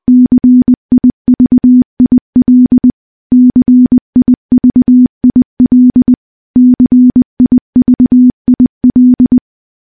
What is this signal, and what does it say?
Morse code, heard as a keyed tone, about 20 wpm, 250 Hz. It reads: CIVIL CIVIL CIVIL
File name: morse-training-20WPM